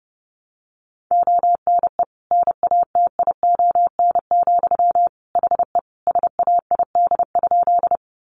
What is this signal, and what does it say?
Morse code, heard as a keyed tone, about 30 wpm, 700 Hz. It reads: ONE NATION, HE SAID?
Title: Sentences from Top 900 Words 30wpm